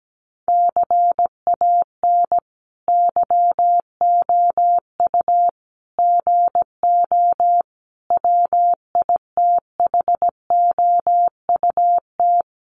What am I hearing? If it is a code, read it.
Morse code, heard as a keyed tone, about 17 wpm, 700 Hz. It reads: CAN YOU GO WITHOUT
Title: Sentences from Top 1500 Words 17wpm